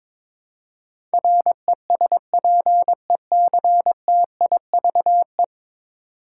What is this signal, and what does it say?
Morse code, heard as a keyed tone, about 22 wpm, 700 Hz. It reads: RESPECTIVE
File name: 10 Letter Words - First 3 Letters Given - 22wpm